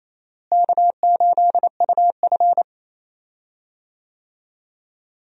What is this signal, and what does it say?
Morse code, heard as a keyed tone, about 28 wpm, 700 Hz. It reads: K8UF
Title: USA Call Signs - 4 Characters - Repeated 2x - 28wpm